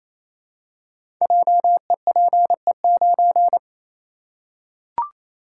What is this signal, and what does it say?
Morse code, heard as a keyed tone, about 27 wpm, 700 Hz. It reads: JEPE9
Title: Code Groups - 5 Letter:Numbers - 28wpm